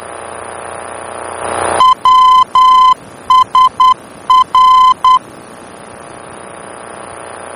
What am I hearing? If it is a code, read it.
Morse code, heard as a keyed tone, about 10 wpm, 1020 Hz. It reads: WSR
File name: germany / mx / WSR-D-VOR